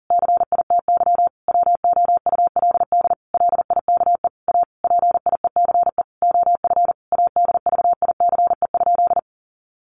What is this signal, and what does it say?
Morse code, heard as a keyed tone, about 40 wpm, 700 Hz. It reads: CITY WOULD LIKE A PIECE OF ADVICE?
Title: Sentences from Top 1100 Words 40wpm